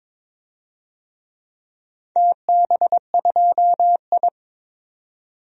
Morse code, heard as a keyed tone, about 22 wpm, 700 Hz. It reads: TB2I
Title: DXCC-100 Call Signs - 4 Characters - Repeated 3x - 22wpm